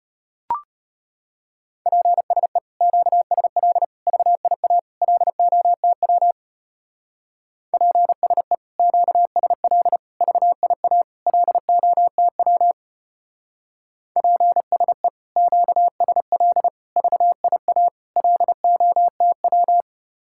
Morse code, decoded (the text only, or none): PSE QSL VIA LOTW PSE QSL VIA LOTW PSE QSL VIA LOTW